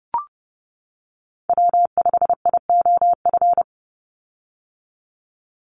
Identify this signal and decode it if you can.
Morse code, heard as a keyed tone, about 30 wpm, 700 Hz. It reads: W5IOF
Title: USA Call Signs 30wpm